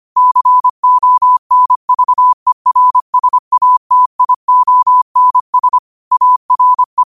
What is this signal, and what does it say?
Morse code, heard as a keyed tone, about 25 wpm, 1000 Hz. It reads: CONVERSATIONS ARE